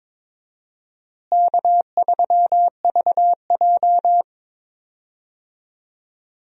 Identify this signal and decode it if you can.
Morse code, heard as a keyed tone, about 22 wpm, 700 Hz. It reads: K3VJ
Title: USA Call Signs - 4 Characters - Repeated 3x - 22wpm